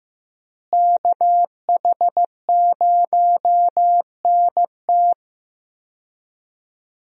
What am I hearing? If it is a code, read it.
Morse code, heard as a keyed tone, about 15 wpm, 700 Hz. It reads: KH0NT
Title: USA Call Signs - 5 Characters - Repeated 2x - 15wpm